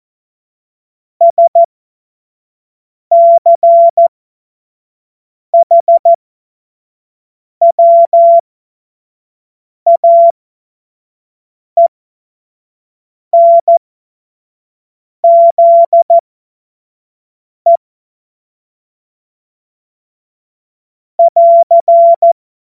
Morse code, heard as a keyed tone, about 14 wpm, spaced as for 5 wpm, 680 Hz. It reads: SCHWAENZE <AR>
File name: Morsekurs / Witze / Witz 05. April 2022 25 F70